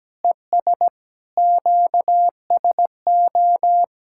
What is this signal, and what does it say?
Morse code, heard as a keyed tone, about 17 wpm, 700 Hz. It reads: ES QSO
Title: Practice QSO - Exchange 4 - Ending - 17wpm